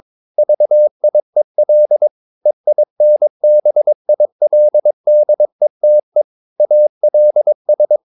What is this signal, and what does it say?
Morse code, heard as a keyed tone, about 22 wpm, 600 Hz. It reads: VIEL EINBILDETE ALS